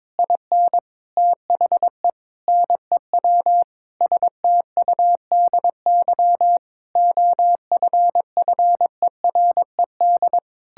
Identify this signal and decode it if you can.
Morse code, heard as a keyed tone, about 22 wpm, 700 Hz. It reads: IN THE NEW STUDY OFFERED
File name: practice / 22wpm / Sentences with 8-Word Phrases - 22wpm